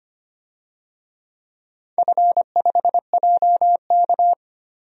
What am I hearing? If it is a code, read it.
Morse code, heard as a keyed tone, about 25 wpm, 700 Hz. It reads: F5JK
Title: DXCC-100 Call Signs - 4 Characters - Repeated 3x - 25wpm